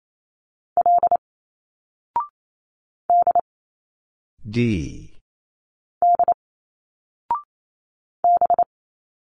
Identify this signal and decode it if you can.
Morse code, heard as a keyed tone, about 28 wpm, 700 Hz. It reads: L D D B
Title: Single Letters 28wpm